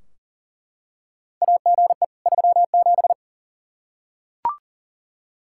Morse code, heard as a keyed tone, about 40 wpm, 700 Hz. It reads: AGE 37
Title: K1USN SST QSO Practice 40wpm